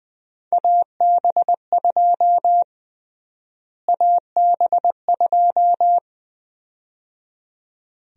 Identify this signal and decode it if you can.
Morse code, heard as a keyed tone, about 21 wpm, 700 Hz. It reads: AB2 AB2
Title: USA Call Sign Prefixes - Repeated 2x - 20wpm